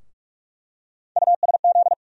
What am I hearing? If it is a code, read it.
Morse code, heard as a keyed tone, about 47 wpm, 700 Hz. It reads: USZ
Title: USA Call Sign Postfixes - 3 Letters 45wpm